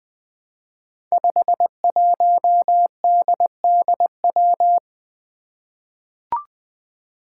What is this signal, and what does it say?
Morse code, heard as a keyed tone, about 20 wpm, 700 Hz. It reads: 51DDW
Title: Code Groups - 5 Letter:Numbers - 20wpm